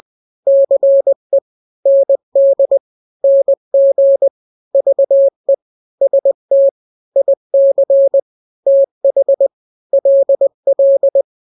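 Morse code, heard as a keyed tone, about 20 wpm, 550 Hz. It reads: CE ND NG VE ST IC TH LL
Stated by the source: Morse drill of two-letter n-grams at 20/18 WPM (Farnsworth), from KK7RR.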